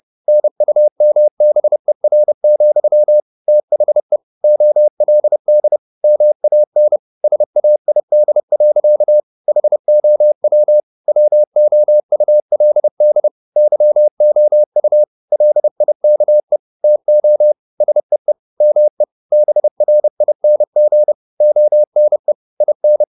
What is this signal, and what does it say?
Morse code, heard as a keyed tone, about 30 wpm, 600 Hz. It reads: NUMBER, THE OLD MAN SAID. HOW WOULD YOU LIKE TO SEE ME BRING ONE IN
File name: the old man and the sea / the old man and the sea 1 30wpm